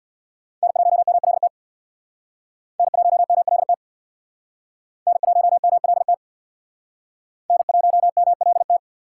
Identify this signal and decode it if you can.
Morse code, heard as a keyed tone, about 67 wpm, 700 Hz. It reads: N1MLT N1MLT N1MLT N1MLT